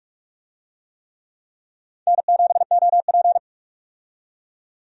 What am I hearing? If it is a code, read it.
Morse code, heard as a keyed tone, about 42 wpm, 700 Hz. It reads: N7OP
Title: USA Call Signs - 4 Characters - 45wpm